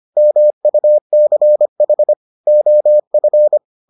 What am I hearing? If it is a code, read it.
Morse code, heard as a keyed tone, about 25 wpm, 600 Hz. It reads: MUCH OF